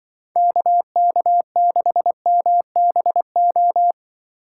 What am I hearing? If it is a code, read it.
Morse code, heard as a keyed tone, about 23 wpm, 700 Hz. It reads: KK6MBO
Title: Speed Racing - USA 2x3 Call Signs 20wpm